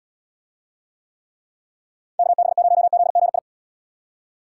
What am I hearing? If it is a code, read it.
Morse code, heard as a keyed tone, about 79 wpm, 700 Hz. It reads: KB0ZGI